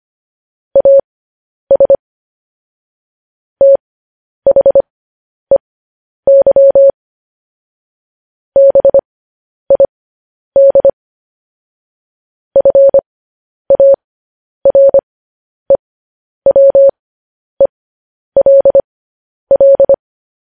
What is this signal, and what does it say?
Morse code, heard as a keyed tone, about 25 wpm, spaced as for 10 wpm, 550 Hz. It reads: AS THEY BID FAREWELL